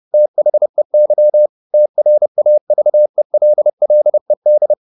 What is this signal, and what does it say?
Morse code, heard as a keyed tone, about 30 wpm, 600 Hz. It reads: THEY TRAVELLED
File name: the old man and the sea / the old man and the sea 1 30wpm